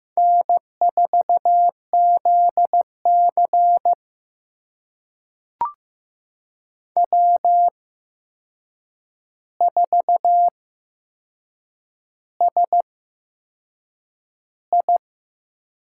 Morse code, heard as a keyed tone, about 15 wpm, spaced as for 6 wpm, 700 Hz. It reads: N4ZC W4SI
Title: USA Call Signs - 4 Characters - 8x Character Spacing - 15wpm